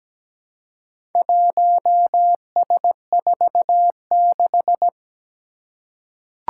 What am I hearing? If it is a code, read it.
Morse code, heard as a keyed tone, about 17 wpm, 700 Hz. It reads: 1S46
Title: Four Challenging Letter-Number Combinations 17wpm